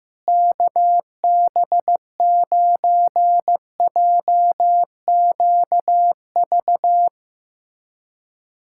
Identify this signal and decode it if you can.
Morse code, heard as a keyed tone, about 15 wpm, 700 Hz. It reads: KB9JQV
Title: USA Callsign DE Callsign 15wpm